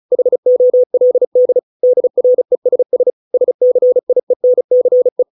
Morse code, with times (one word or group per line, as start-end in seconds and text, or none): 0.1-1.6 HOLD
1.8-3.1 DRESS
3.3-5.2 SCIENCE